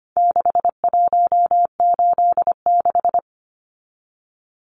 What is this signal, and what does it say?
Morse code, heard as a keyed tone, about 25 wpm, 700 Hz. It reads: 6186
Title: Numbers - 4-Digit 25wpm